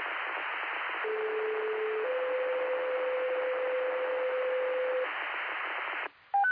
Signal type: two-tone page